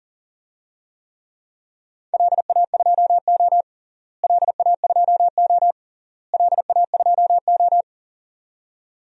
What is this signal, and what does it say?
Morse code, heard as a keyed tone, about 40 wpm, 700 Hz. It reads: LA2O LA2O LA2O
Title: DXCC-100 Call Signs - 4 Characters - Repeated 3x - 40wpm